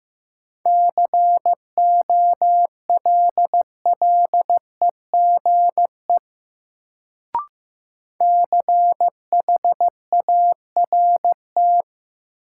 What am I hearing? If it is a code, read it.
Morse code, heard as a keyed tone, about 16 wpm, 700 Hz. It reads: COLLEGE CHART
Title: Sentences from Top 1300 Words - New Word Drill Alt - 15wpm